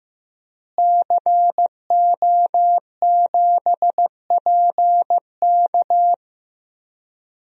Morse code, heard as a keyed tone, about 15 wpm, 700 Hz. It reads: CO7PK
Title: DXCC-100 Call Signs - 5 Characters - Repeated 2x - 15wpm